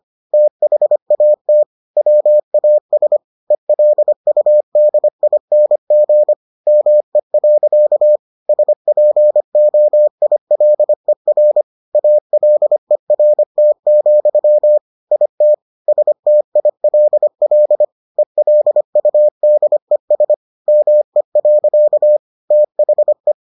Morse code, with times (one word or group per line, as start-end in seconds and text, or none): 0.3-1.6 THAT
2.0-3.2 WAS
3.5-6.3 ELUDING
6.7-8.2 ME.
8.5-11.6 SPOILER
11.9-14.8 ALERT,
15.1-15.5 IT
15.9-17.8 STILL
18.2-20.3 ELUDES
20.7-22.2 ME.
22.5-23.3 THE